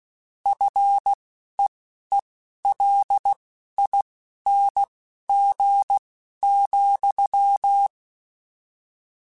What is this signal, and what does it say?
Morse code, heard as a keyed tone, about 16 wpm, spaced as for 11 wpm, 800 Hz. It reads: FEELING,